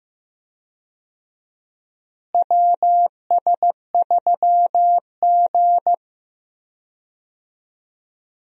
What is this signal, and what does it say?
Morse code, heard as a keyed tone, about 15 wpm, 700 Hz. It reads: WS3G
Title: USA Call Signs - 4 Characters - Repeated 2x - 15wpm